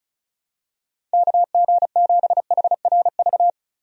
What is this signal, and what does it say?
Morse code, heard as a keyed tone, about 34 wpm, 700 Hz. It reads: KG7HRV
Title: USA Call Signs - 6 Characters - 35wpm